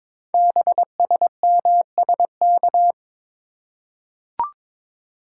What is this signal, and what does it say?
Morse code, heard as a keyed tone, about 22 wpm, 700 Hz. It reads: BSMSK
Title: Code Groups - 5 Letters - 22wpm